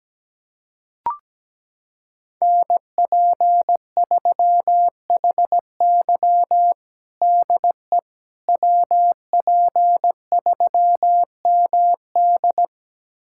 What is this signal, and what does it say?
Morse code, heard as a keyed tone, about 17 wpm, 700 Hz. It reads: NP3HY DE WP3MD
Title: USA Callsign DE Callsign - 5 Characters - 17wpm